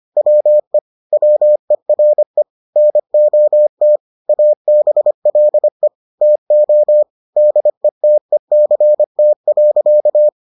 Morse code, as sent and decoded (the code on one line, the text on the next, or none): .-- . / .-- . .-. / -. --- - / .- -... .-.. / - --- / -.. - . -.-. - .-.-.-
WE WERE NOT ABLE TO DETECT.